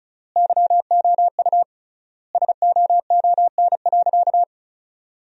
Morse code, heard as a keyed tone, about 36 wpm, 700 Hz. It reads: YOU SOON.
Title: 3x Word-Spacing - Sentences from Top 400 Words - 35wpm